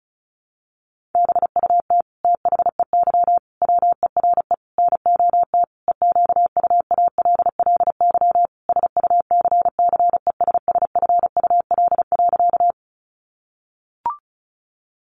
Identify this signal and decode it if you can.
Morse code, heard as a keyed tone, about 34 wpm, 700 Hz. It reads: BUT THEY WERE NOT EQUALLY SUCCESSFUL.
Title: Sentences from Top 1400 Words 35wpm